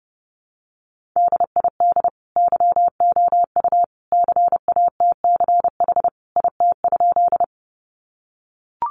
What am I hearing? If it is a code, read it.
Morse code, heard as a keyed tone, about 30 wpm, 700 Hz. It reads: DID YOU CATCH IT?
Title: Sentences from Top 1500 Words 30wpm